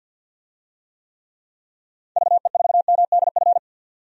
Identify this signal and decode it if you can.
Morse code, heard as a keyed tone, about 52 wpm, 700 Hz. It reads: VE4MDF
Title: DXCC-100 Call Signs - 6 Characters - 50wpm